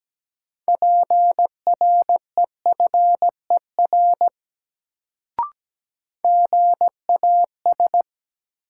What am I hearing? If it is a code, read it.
Morse code, heard as a keyed tone, about 17 wpm, 700 Hz. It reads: PREFER GAS
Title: Sentences from Top 1100 Words - New Word Drill - 17wpm